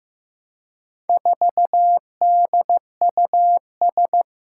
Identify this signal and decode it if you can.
Morse code, heard as a keyed tone, about 15 wpm, 700 Hz. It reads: 4DUS
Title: Four Challenging Letter-Number Combinations 15wpm